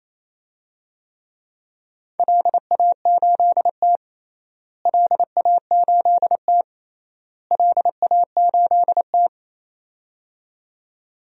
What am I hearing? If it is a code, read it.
Morse code, heard as a keyed tone, about 28 wpm, 700 Hz. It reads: LA8T LA8T LA8T
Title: DXCC-100 Call Signs - 4 Characters - Repeated 3x - 28wpm